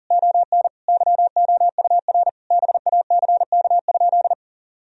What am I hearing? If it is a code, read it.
Morse code, heard as a keyed tone, about 37 wpm, 700 Hz. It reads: ON YOUR BACK?
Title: Sentences from Top 1000 Words - Review Part 2 - 40wpm